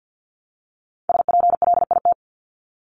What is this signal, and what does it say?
Morse code, heard as a keyed tone, about 50 wpm, 700 Hz. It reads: SPLIT
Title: Top 1751-2000 Words 50wpm